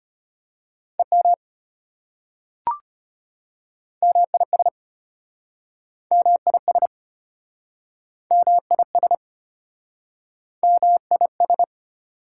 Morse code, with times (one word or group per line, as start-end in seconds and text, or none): 1.0-1.3 EM
4.0-4.7 MIS
6.1-6.9 MIS
8.3-9.1 MIS
10.6-11.6 MIS